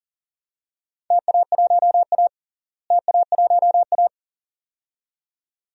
Morse code, heard as a keyed tone, about 40 wpm, 700 Hz. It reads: TA1A TA1A
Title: DXCC-100 Call Signs - 4 Characters - Repeated 2x - 40wpm